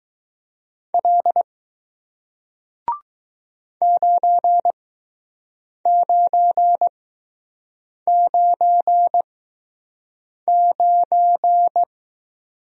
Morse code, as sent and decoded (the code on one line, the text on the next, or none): .-.. / ----. ----. ----. ----.
L 9999